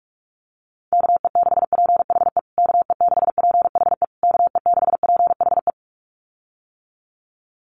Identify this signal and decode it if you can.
Morse code, heard as a keyed tone, about 44 wpm, 700 Hz. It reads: KE6PHE KE6PHE KE6PHE
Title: USA Call Signs Repeated 3x 45wpm